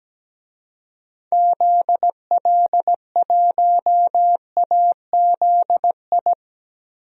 Morse code, heard as a keyed tone, about 17 wpm, 700 Hz. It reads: ZL1AZI
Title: DXCC-100 Call Signs - 6 Characters - 17wpm